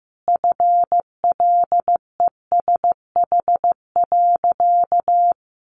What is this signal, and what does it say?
Morse code, heard as a keyed tone, about 15 wpm, 700 Hz. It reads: FLESH.
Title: A Sound Of Thunder 15wpm